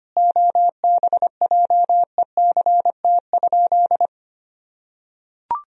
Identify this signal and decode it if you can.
Morse code, heard as a keyed tone, about 25 wpm, 700 Hz. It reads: OBJECT?
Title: Sentences from Top 1000 Words - Review Part 3 - 25wpm